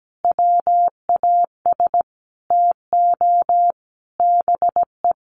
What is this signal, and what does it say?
Morse code, heard as a keyed tone, about 17 wpm, 700 Hz. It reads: WAS TO BE
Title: Sentences from Top 1400 Words 17wpm